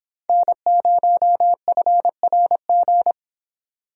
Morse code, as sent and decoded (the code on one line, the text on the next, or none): -. ----- ..-. .-. --.
N0FRG